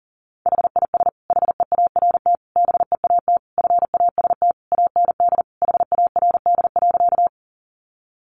Morse code, heard as a keyed tone, about 40 wpm, 700 Hz. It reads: HIS HEART BEAT FAST AND HARD.
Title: Sentences from Top 1300 Words 40wpm